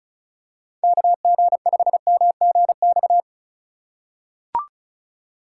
Morse code, heard as a keyed tone, about 36 wpm, 700 Hz. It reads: KG5MGX
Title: USA Call Signs - 6 Characters - 4x Character Spacing - 35wpm